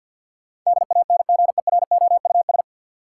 Morse code, heard as a keyed tone, about 48 wpm, 700 Hz. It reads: DANGEROUS
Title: Sentences from Top 1200 Words - New Word Drill - 50wpm